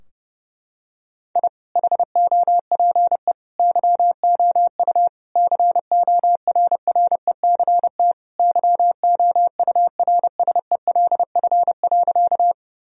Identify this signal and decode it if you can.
Morse code, heard as a keyed tone, about 32 wpm, 700 Hz. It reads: I HOPE YOU CORRECT YOURSELF.
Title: Sentences from Top 1000 Words - Review Part 2 - 30wpm